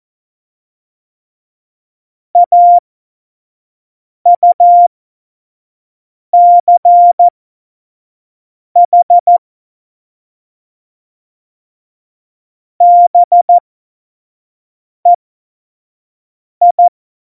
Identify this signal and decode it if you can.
Morse code, heard as a keyed tone, about 14 wpm, spaced as for 5 wpm, 700 Hz. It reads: AUCH BEI
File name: Morsekurs / Witze / Witz 25. Juni 2019, 25 F70